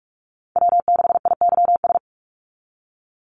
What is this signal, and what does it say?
Morse code, heard as a keyed tone, about 46 wpm, 700 Hz. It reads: W6IYS
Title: NA SOTA Chaser Callsigns 45wpm